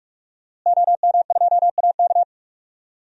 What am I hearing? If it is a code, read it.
Morse code, heard as a keyed tone, about 46 wpm, 700 Hz. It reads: OM2AK
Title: DXCC-100 Call Signs - 5 Characters - Repeated 2x - 45wpm